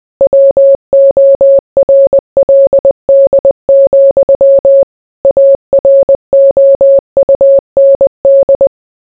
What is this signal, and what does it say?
Morse code, heard as a keyed tone, about 20 wpm, 550 Hz. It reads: WORLD, AROUND